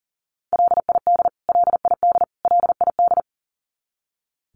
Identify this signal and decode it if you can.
Morse code, heard as a keyed tone, about 40 wpm, 700 Hz. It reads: LID LID LID
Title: Abbreviations Repeated 3x - 40wpm